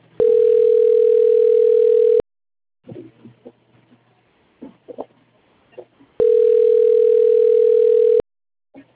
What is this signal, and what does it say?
Ringback tone (North America), 440 + 480 Hz.
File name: out- - - - - .31891